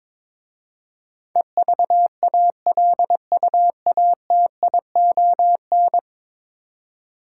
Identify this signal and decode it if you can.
Morse code, heard as a keyed tone, about 22 wpm, 700 Hz. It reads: EVALUATION
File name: practice / 22wpm / 10 Letter Words - First 3 Letters Given - 22wpm